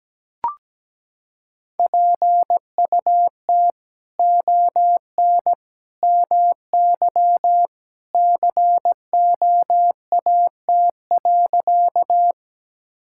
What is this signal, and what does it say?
Morse code, heard as a keyed tone, about 17 wpm, 700 Hz. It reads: PUT ON MY COAT.
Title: Sentences from Top 1600 Words 17wpm